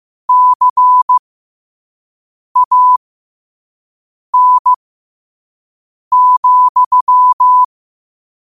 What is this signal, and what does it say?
Morse code, heard as a keyed tone, about 15 wpm, spaced as for 5 wpm, 1000 Hz. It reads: CAN,
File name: qotdfiles / 5wpm